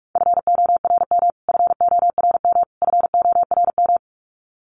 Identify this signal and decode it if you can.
Morse code, heard as a keyed tone, about 44 wpm, 700 Hz. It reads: FORM FORM FORM
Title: Top 300 Words Repeated 45wpm